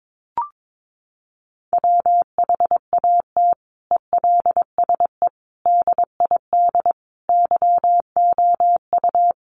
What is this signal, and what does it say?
Morse code, heard as a keyed tone, about 22 wpm, 700 Hz. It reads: WHAT ELSE DID YOU
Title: Sentences from Top 800 Words 22wpm